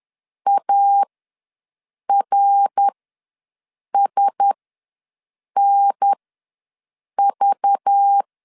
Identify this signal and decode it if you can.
Morse code, heard as a keyed tone, about 10 wpm, spaced as for 6 wpm, 800 Hz. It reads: ARSNV
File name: LEZIONI / 9 2 ws 10wpm 4130